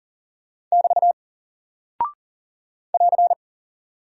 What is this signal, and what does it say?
Morse code, heard as a keyed tone, about 40 wpm, 700 Hz. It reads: <BT> <AR>
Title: Single Character (Letter:Number:Punctuation:Prosign) Alt 40wpm